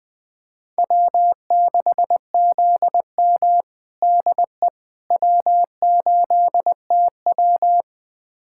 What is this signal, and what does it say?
Morse code, heard as a keyed tone, about 21 wpm, 700 Hz. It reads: W6ZM DE W8TW
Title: USA Callsign DE Callsign - 4 Characters - 20wpm